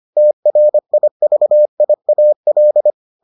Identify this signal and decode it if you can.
Morse code, heard as a keyed tone, about 25 wpm, 600 Hz. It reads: TRIVIAL